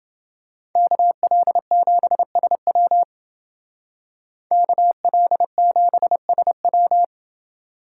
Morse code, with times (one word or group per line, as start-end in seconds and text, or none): 0.8-3.0 KL7SW
4.5-7.0 KL7SW